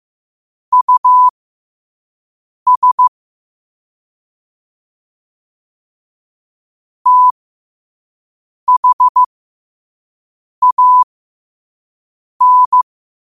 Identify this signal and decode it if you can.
Morse code, heard as a keyed tone, about 15 wpm, spaced as for 5 wpm, 1000 Hz. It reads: US THAN
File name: qotdfiles / 5wpm